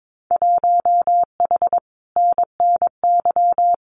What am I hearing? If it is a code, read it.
Morse code, heard as a keyed tone, about 22 wpm, 700 Hz. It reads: 1H NNY
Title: Winter Field Day Exchanges 22wpm